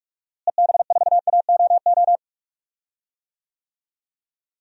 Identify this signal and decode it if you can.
Morse code, heard as a keyed tone, about 43 wpm, 700 Hz. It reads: EB4AOO
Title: DXCC-100 Call Signs - 6 Characters - Repeated 3x - 45wpm